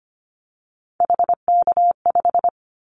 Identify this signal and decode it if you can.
Morse code, heard as a keyed tone, about 25 wpm, 700 Hz. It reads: HK5
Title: Callsign Prefixes for Top 100 DXCC - 25wpm